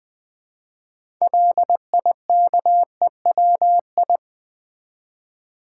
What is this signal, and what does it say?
Morse code, heard as a keyed tone, about 20 wpm, 700 Hz. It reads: LIKEWI
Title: Build A Word - 2 Letters At A Time - 8-Letter Words - 20wpm